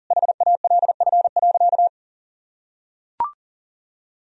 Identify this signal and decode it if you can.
Morse code, heard as a keyed tone, about 40 wpm, 700 Hz. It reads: HALF.
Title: Sentences from Top 1000 Words - Review Part 4 - 40wpm